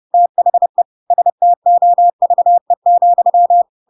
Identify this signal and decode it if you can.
Morse code, heard as a keyed tone, about 30 wpm, 700 Hz. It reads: THE STOVE,
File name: Pinocchio / Pinocchio ch1 30 30 ik1hge